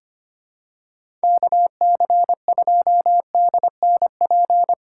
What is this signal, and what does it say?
Morse code, heard as a keyed tone, about 25 wpm, 700 Hz. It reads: KC2DNP
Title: USA Call Signs - 6 Characters - 3x Character Spacing - 25wpm